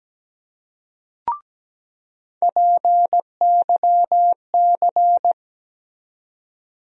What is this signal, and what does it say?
Morse code, heard as a keyed tone, about 17 wpm, 700 Hz. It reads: PYC
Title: USA Call Sign Postfixes - 3 Letters 17wpm